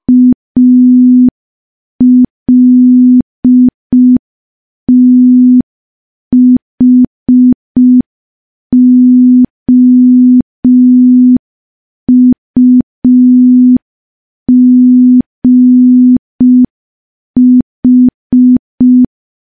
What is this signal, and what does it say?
Morse code, heard as a keyed tone, about 5 wpm, 250 Hz. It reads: ALTHOUGH
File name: morse / morse-training-05WPM